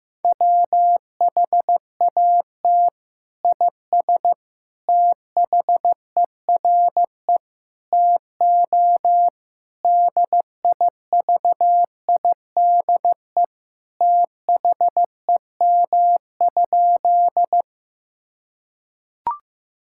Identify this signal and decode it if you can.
Morse code, heard as a keyed tone, about 15 wpm, 700 Hz. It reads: WHAT IS THERE TO DIVIDE THEM?
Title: Sentences from Top 1600 Words 15wpm